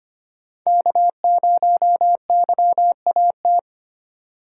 Morse code, heard as a keyed tone, about 25 wpm, 700 Hz. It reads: K0YAT
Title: USA Call Signs - 5 Characters - 2x Character Spacing - 25wpm